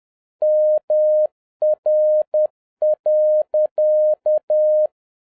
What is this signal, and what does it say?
Morse code, heard as a keyed tone, about 10 wpm, 610 Hz. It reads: MR.
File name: 2011-07-mp3 / n31 THE 3.11 2 10wpm